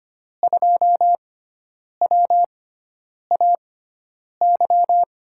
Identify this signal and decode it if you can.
Morse code, heard as a keyed tone, about 25 wpm, spaced as for 9 wpm, 700 Hz. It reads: 2WAY
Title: USA Call Signs - 5 Characters - 6x Character Spacing - 25wpm